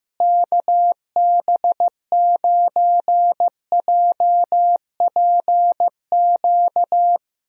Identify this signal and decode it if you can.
Morse code, heard as a keyed tone, about 15 wpm, 700 Hz. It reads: KB9JPQ
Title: USA Call Signs - 6 Characters - Repeated 3x - 15wpm